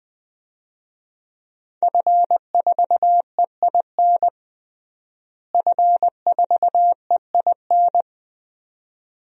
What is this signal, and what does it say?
Morse code, heard as a keyed tone, about 20 wpm, 700 Hz. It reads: F4EIN F4EIN
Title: DXCC-100 Call Signs - 5 Characters - Repeated 2x - 20wpm